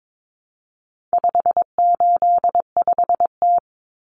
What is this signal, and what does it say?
Morse code, heard as a keyed tone, about 22 wpm, 700 Hz. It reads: 585T
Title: Common Cut Numbers - 4-Digit 22wpm